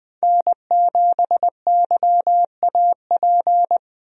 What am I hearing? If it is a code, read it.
Morse code, heard as a keyed tone, about 20 wpm, 700 Hz. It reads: N7YAP
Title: USA Call Signs - 5 Characters - 20wpm